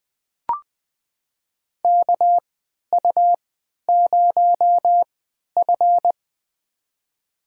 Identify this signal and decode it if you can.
Morse code, heard as a keyed tone, about 20 wpm, spaced as for 11 wpm, 700 Hz. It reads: KU0F
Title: USA Call Signs - 4 Characters - 3x Character Spacing - 20wpm